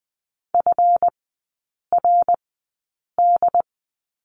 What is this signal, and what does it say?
Morse code, heard as a keyed tone, about 20 wpm, spaced as for 8 wpm, 690 Hz. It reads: FRD
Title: 145 Lesson 24e - Letter y - Three Characters 20wpm